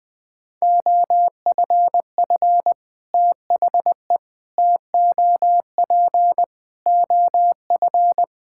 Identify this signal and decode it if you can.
Morse code, heard as a keyed tone, about 21 wpm, 700 Hz. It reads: OFF THE TOP OF
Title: Sets of 6 Words - 20wpm